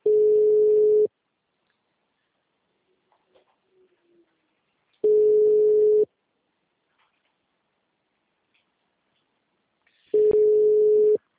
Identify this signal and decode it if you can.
Ringback tone (Europe, 425 Hz).